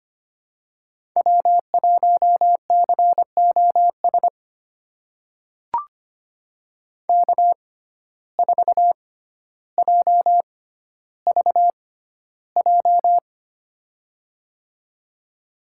Morse code, heard as a keyed tone, about 25 wpm, spaced as for 13 wpm, 700 Hz. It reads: W1COS K4JVJ
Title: USA Call Signs - 5 Characters - 6x Character Spacing - 25wpm